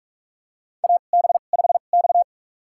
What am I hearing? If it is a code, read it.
Morse code, heard as a keyed tone, about 44 wpm, spaced as for 32 wpm, 690 Hz. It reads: AB5X
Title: USA Call Signs - 4 Characters - 2x Character Spacing - 45wpm